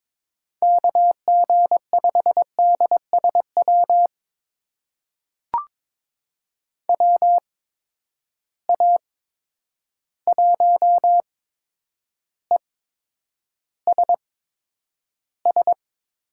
Morse code, heard as a keyed tone, about 22 wpm, spaced as for 9 wpm, 700 Hz. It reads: KG5DSW WA1ESS
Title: USA Call Signs - 6 Characters - 8x Character Spacing - 22wpm